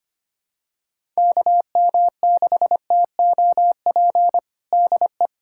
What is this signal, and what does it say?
Morse code, heard as a keyed tone, about 25 wpm, 700 Hz. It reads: KM6TOP DE
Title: USA Callsign DE Callsign - 6 Characters - 25wpm